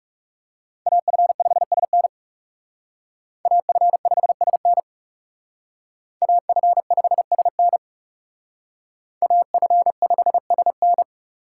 Morse code, heard as a keyed tone, about 36 wpm, 700 Hz. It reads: AF5SN AF5SN AF5SN AF5SN
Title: Speed Racing - USA 2x2 Call Signs 30wpm